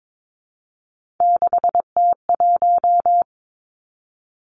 Morse code, heard as a keyed tone, about 22 wpm, 700 Hz. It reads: 6T1
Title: Three Challenging Letter-Number Combinations 22wpm